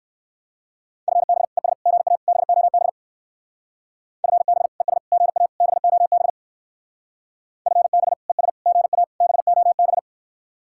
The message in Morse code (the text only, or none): FB ES GA BOB FB ES GA BOB FB ES GA BOB